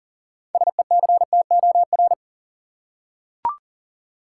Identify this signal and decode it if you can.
Morse code, heard as a keyed tone, about 40 wpm, 700 Hz. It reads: SECTOR
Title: Six Letter Words - 40wpm